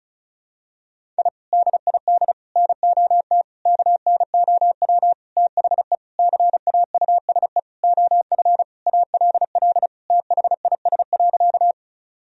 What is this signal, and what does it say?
Morse code, heard as a keyed tone, about 36 wpm, 700 Hz. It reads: I DID NOT KNOW THE CAUSE OF ALL THIS.
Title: Sentences from Top 1000 Words - Review Part 1 - 35wpm